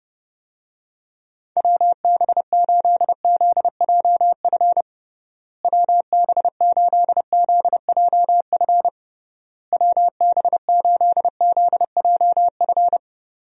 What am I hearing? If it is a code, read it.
Morse code, heard as a keyed tone, about 30 wpm, 700 Hz. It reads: WB8ZJF WB8ZJF WB8ZJF
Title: USA Call Signs - 6 Characters - Repeated 3x - 30wpm